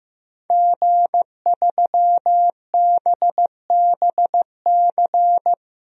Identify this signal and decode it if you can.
Morse code, heard as a keyed tone, about 15 wpm, 700 Hz. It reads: G3BBC
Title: DXCC-100 Call Signs - 5 Characters - Repeated 3x - 15wpm